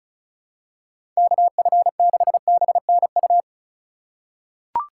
Morse code, heard as a keyed tone, about 36 wpm, 700 Hz. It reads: KF6BNU
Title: USA Call Signs - 3x Character Spacing - 35wpm